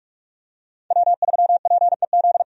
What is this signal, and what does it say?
Morse code, heard as a keyed tone, about 46 wpm, 700 Hz. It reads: W3PEZ
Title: USA Call Signs - 5 Characters - 45wpm